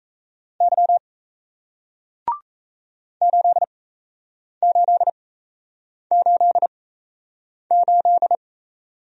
Morse code, heard as a keyed tone, about 38 wpm, spaced as for 8 wpm, 700 Hz. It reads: Y 8888